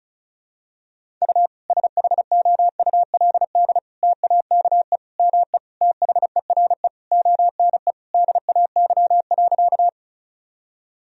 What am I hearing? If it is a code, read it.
Morse code, heard as a keyed tone, about 36 wpm, 700 Hz. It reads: U SHOULD TAKE ME THERE ONE DAY.
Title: Sentences from Top 300 Words and Abbreviations 35wpm